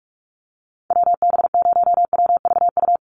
Morse code, heard as a keyed tone, about 44 wpm, 700 Hz. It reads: WB0WVU
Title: USA Call Signs Repeated 2x 45wpm